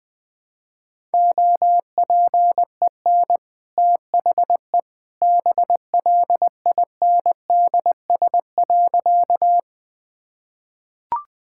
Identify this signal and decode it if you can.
Morse code, heard as a keyed tone, about 21 wpm, 700 Hz. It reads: OPEN THE BLINDS.
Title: Easy Sentences with Hints - 20wpm